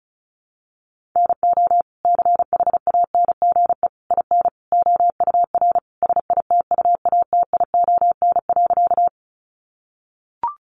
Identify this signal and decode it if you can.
Morse code, heard as a keyed tone, about 34 wpm, 700 Hz. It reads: NO CHANGE IN OUR SITUATION.
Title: Sentences from Top 1200 Words 35wpm